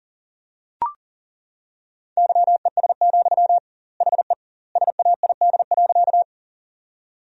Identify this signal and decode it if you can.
Morse code, heard as a keyed tone, about 37 wpm, 700 Hz. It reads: YES, HE SAID.
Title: 2x Word-Spacing - Sentences from Top 500 Words - 40wpm